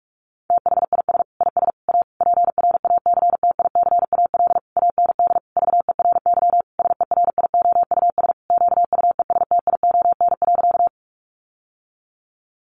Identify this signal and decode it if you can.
Morse code, heard as a keyed tone, about 44 wpm, 700 Hz. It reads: THIS IS A PRACTICAL AND VERY SERIOUS QUESTION.
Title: Sentences from Top 1400 Words 45wpm